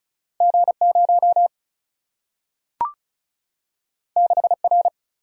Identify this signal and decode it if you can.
Morse code, heard as a keyed tone, about 36 wpm, 700 Hz. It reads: G0 6R
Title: Single-Letter Single-Number 35wpm